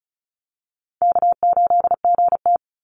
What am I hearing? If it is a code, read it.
Morse code, heard as a keyed tone, about 34 wpm, 700 Hz. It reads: K8GT
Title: Easy Call Signs Repeated 2x - 35wpm